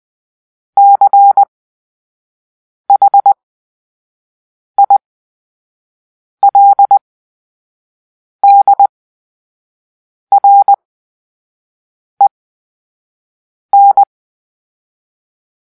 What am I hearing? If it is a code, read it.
Morse code, heard as a keyed tone, about 20 wpm, spaced as for 5 wpm, 800 Hz. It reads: CHILDREN
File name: Grp-3 Drill-4